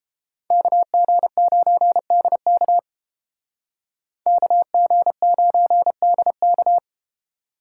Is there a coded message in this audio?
Yes: Morse code from 0.5 to 6.8 s, heard as a keyed tone, about 30 wpm, 700 Hz. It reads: KG9DK KG9DK